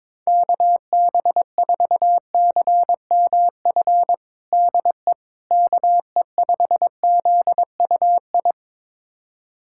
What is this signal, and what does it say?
Morse code, heard as a keyed tone, about 22 wpm, 700 Hz. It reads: KB4CMF DE KE5ZUI
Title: USA Callsign DE Callsign 22wpm